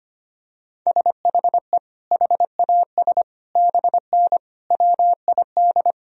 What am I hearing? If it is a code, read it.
Morse code, heard as a keyed tone, about 25 wpm, 700 Hz. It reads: SHE HAS BN WID
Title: Sentences from Top 200 Words and Abbreviations 25wpm